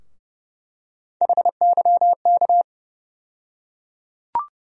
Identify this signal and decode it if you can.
Morse code, heard as a keyed tone, about 29 wpm, 700 Hz. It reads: HYK
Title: USA Call Sign Postfixes - 3 Letters 30wpm